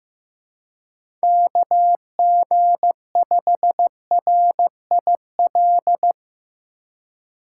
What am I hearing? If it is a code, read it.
Morse code, heard as a keyed tone, about 15 wpm, 700 Hz. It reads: KG5RIL
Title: USA Call Signs - 6 Characters - 4x Character Spacing - 15wpm